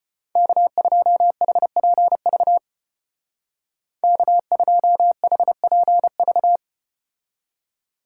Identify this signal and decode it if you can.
Morse code, heard as a keyed tone, about 34 wpm, 700 Hz. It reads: K2HPV K2HPV